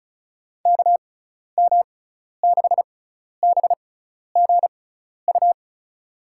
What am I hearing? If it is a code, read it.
Morse code, heard as a keyed tone, about 36 wpm, spaced as for 12 wpm, 700 Hz. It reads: KM6BGU
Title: USA Call Signs - 6 Characters - 6x Character Spacing - 35wpm